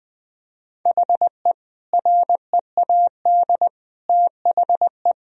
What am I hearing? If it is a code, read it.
Morse code, heard as a keyed tone, about 20 wpm, 700 Hz. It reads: HE READ THE
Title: Sentences from Top 1000 Words - Review Part 1 - 20wpm